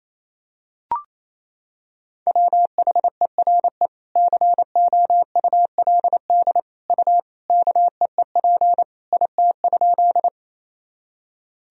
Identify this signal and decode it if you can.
Morse code, heard as a keyed tone, about 28 wpm, 700 Hz. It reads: WHERE COULD U KEEP IT?
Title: Sentences from Top 300 Words and Abbreviations 28wpm